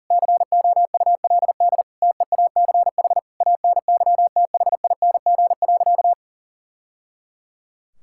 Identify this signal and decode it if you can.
Morse code, heard as a keyed tone, about 40 wpm, 700 Hz. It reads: COULD TEACH ANYTHING.
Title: Sentences from Top 1000 Words - Review Part 6 - 40wpm